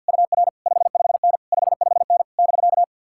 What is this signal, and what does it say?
Morse code, heard as a keyed tone, about 66 wpm, spaced as for 57 wpm, 700 Hz. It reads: UR 55N 55N <BK>
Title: Speed-Racing Common QSO Phrases - 50wpm